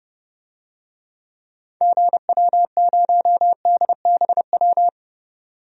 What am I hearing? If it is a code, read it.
Morse code, heard as a keyed tone, about 30 wpm, 700 Hz. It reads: GW0DBW
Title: DXCC-100 Call Signs - 6 Characters - 30wpm